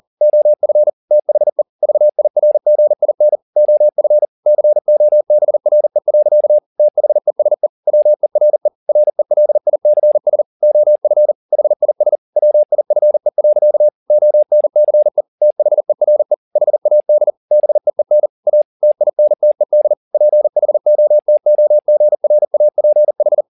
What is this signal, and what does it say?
Morse code, heard as a keyed tone, about 40 wpm, 600 Hz. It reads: OF THE VIRGIN OF COBRE. THESE WERE RELICS OF HIS WIFE. ONCE THERE HAD BEEN A TINTED PHOTOGRAPH